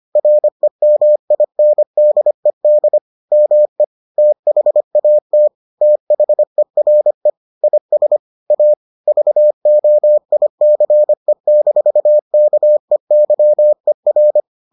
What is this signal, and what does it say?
Morse code, heard as a keyed tone, about 25 wpm, 600 Hz. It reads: REMINDED ME THAT THERE IS A VOICE-KEYER